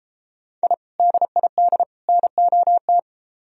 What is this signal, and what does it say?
Morse code, heard as a keyed tone, about 33 wpm, 700 Hz. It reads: I DID NOT